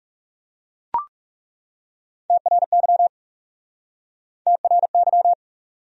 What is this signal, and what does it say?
Morse code, heard as a keyed tone, about 37 wpm, 700 Hz. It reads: TRY TRY
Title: Speed-Racing Top 200 Words 30wpm